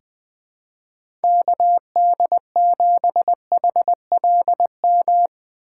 Morse code, heard as a keyed tone, about 21 wpm, 700 Hz. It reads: KD7HLM
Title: USA Call Signs - 6 Characters - 2x Character Spacing - 20wpm